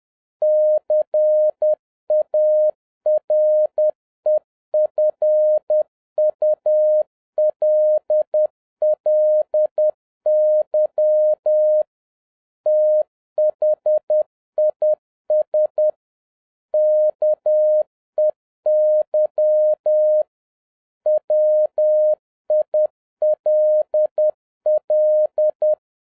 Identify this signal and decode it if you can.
Morse code, heard as a keyed tone, about 10 wpm, 610 Hz. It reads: CAREFULLY THIS KEY WILL